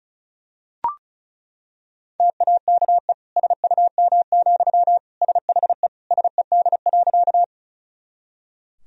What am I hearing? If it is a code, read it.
Morse code, heard as a keyed tone, about 36 wpm, 700 Hz. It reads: TAKE SUM, SHE SED.
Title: Sentences from Top 200 Words and Abbreviations 35wpm